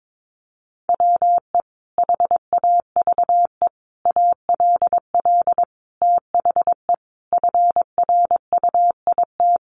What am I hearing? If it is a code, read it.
Morse code, heard as a keyed tone, about 22 wpm, 700 Hz. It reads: WE HAVE ALL THE FRUIT
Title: Sentences from Top 1300 Words 22wpm